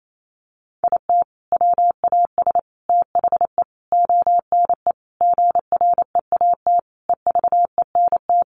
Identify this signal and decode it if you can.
Morse code, heard as a keyed tone, about 28 wpm, 700 Hz. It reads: IT WAS THE ONE GREAT EVENT
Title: Sentences from Top 1400 Words 28wpm